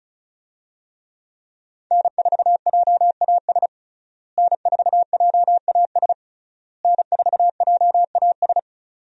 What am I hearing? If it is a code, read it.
Morse code, heard as a keyed tone, about 36 wpm, 700 Hz. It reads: N4JAS N4JAS N4JAS
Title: USA Call Signs - 5 Characters - Repeated 3x - 35wpm